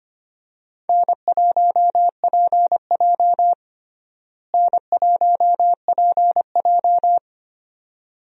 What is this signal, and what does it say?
Morse code, heard as a keyed tone, about 25 wpm, 700 Hz. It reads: N1PJ N1PJ
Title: USA Call Signs - 4 Characters - Repeated 2x - 25wpm